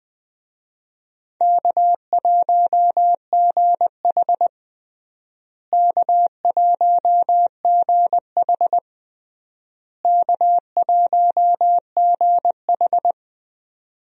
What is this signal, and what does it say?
Morse code, heard as a keyed tone, about 20 wpm, 700 Hz. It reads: K1GH K1GH K1GH
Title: USA Call Signs - 4 Characters - Repeated 3x - 20wpm